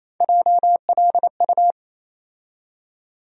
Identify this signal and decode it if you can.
Morse code, heard as a keyed tone, about 28 wpm, 700 Hz. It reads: JLU
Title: USA Call Sign Postfixes - 3 Letters 28wpm